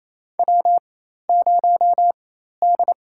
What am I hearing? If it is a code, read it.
Morse code, heard as a keyed tone, about 28 wpm, spaced as for 13 wpm, 700 Hz. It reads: W0D
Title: USA Call Signs - 4x Character Spacing - 28wpm